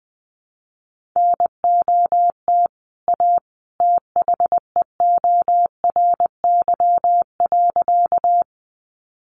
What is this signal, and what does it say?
Morse code, heard as a keyed tone, about 20 wpm, 700 Hz. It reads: NOT A THEORY.
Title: Sentences from Top 600 Words 20wpm